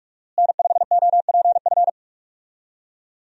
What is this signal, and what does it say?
Morse code, heard as a keyed tone, about 47 wpm, 700 Hz. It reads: N5OPF
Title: USA Call Signs - 5 Characters - Repeated 2x - 45wpm